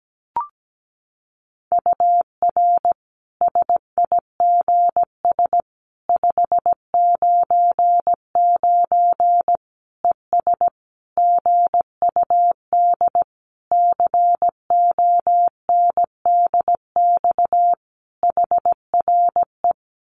Morse code, heard as a keyed tone, about 17 wpm, 700 Hz. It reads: UR SIGS 599 ES GUD CONDX HRE
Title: Common QSO Phrases 17wpm